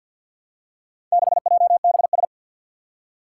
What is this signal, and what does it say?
Morse code, heard as a keyed tone, about 53 wpm, 700 Hz. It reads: 6JBS
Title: Four Challenging Letter-Number Combinations 50wpm